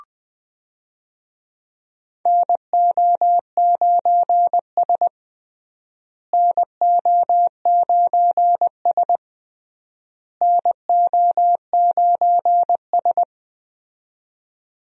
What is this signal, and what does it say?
Morse code, heard as a keyed tone, about 19 wpm, 700 Hz. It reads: NO9S NO9S NO9S
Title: USA Call Signs - 4 Characters - Repeated 3x - 20wpm